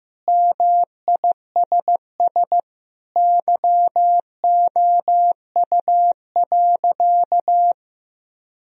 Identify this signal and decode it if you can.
Morse code, heard as a keyed tone, about 15 wpm, 700 Hz. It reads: MISS YOU.
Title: Sentences from Top 1000 Words - Review Part 3 - 15wpm